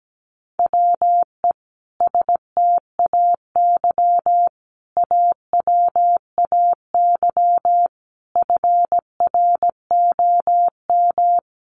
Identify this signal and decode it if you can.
Morse code, heard as a keyed tone, about 17 wpm, 700 Hz. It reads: WE STAY AWAY FROM
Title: A Sound Of Thunder 17wpm